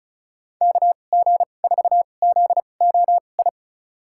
Morse code, heard as a keyed tone, about 36 wpm, spaced as for 25 wpm, 700 Hz. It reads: KG4ZOI
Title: USA Call Signs - 6 Characters - 2x Character Spacing - 35wpm